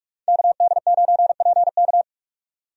Morse code, heard as a keyed tone, about 44 wpm, 700 Hz. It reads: KD9PK
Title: USA Call Signs - 5 Characters - 45wpm